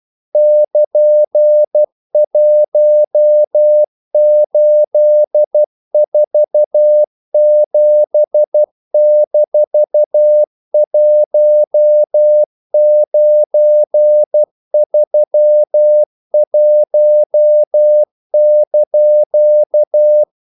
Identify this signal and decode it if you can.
Morse code, heard as a keyed tone, about 12 wpm, 600 Hz. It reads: <KN>1847-1931)
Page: English quotations in CW (telegraphy - morse)